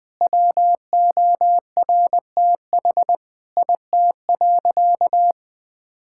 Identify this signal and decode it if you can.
Morse code, heard as a keyed tone, about 20 wpm, 700 Hz. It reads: WORTH IT.
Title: Sentences from Top 1000 Words - Review Part 5 - 20wpm